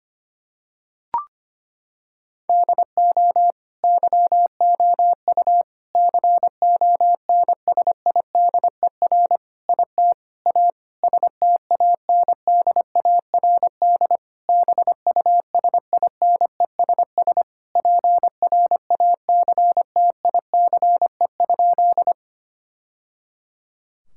Morse code, heard as a keyed tone, about 25 wpm, 700 Hz. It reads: DO YOU CONSIDER IT A STANDARD BUSINESS PRACTICE?
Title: Sentences from Top 1000 Words - Review Part 6 - 25wpm